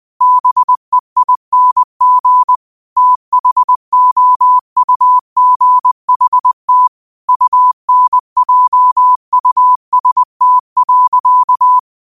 Morse code, heard as a keyed tone, about 20 wpm, 1000 Hz. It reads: BEING THOUGHT UNJUST.
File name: qotdfiles / 20wpm